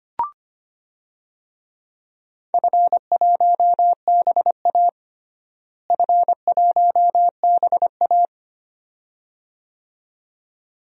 Morse code, heard as a keyed tone, about 25 wpm, 700 Hz. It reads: F1BA F1BA
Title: DXCC-100 Call Signs - 4 Characters - Repeated 2x - 25wpm